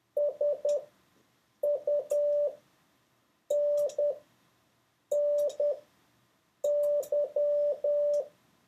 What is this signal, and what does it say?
Morse code, heard as a keyed tone, about 10 wpm, spaced as for 6 wpm, 580 Hz. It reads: SUNNY